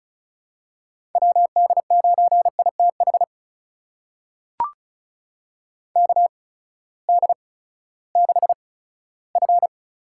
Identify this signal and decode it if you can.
Morse code, heard as a keyed tone, about 34 wpm, spaced as for 17 wpm, 700 Hz. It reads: WD9ITH KD6F
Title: USA Call Signs - 8x Character Spacing - 35wpm